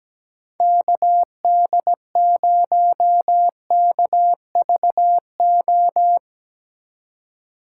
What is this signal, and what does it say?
Morse code, heard as a keyed tone, about 17 wpm, 700 Hz. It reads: KD0KVO
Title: USA Call Signs - 3x Character Spacing - 17wpm